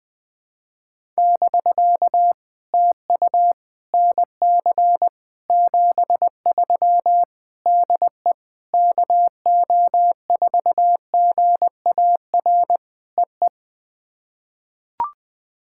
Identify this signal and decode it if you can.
Morse code, heard as a keyed tone, about 21 wpm, 690 Hz. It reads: <BK> TU NC 73 DE KO4GAR EE
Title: POTA QSO Practice - 20wpm